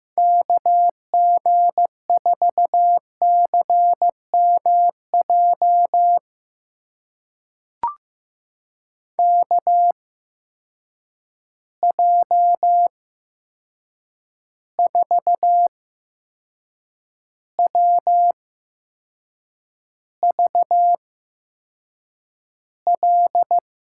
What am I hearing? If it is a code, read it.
Morse code, heard as a keyed tone, about 16 wpm, spaced as for 6 wpm, 700 Hz. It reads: KG4CMJ KJ4WVL
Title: USA Call Signs - 6 Characters - 8x Character Spacing - 15wpm